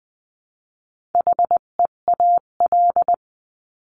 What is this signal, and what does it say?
Morse code, heard as a keyed tone, about 20 wpm, 700 Hz. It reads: HEAL
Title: 258 Lesson 42c - Words 16-20wpm